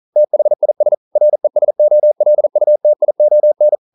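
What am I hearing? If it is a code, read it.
Morse code, heard as a keyed tone, about 42 wpm, 600 Hz. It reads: THIS RESOLUTION
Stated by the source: Morse code practice at 41 wpm, from CW med Gnister.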